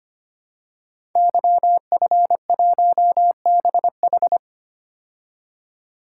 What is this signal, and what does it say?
Morse code, heard as a keyed tone, about 25 wpm, 700 Hz. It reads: YF1BH
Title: Code Groups - 5 Letter:Numbers - 25wpm